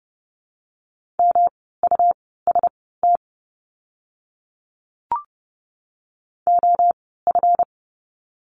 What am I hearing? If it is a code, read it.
Morse code, heard as a keyed tone, about 30 wpm, spaced as for 17 wpm, 700 Hz. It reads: MUST OF
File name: course / 30:10wpm / 237 Lesson 38f - Letter q - Words 30wpm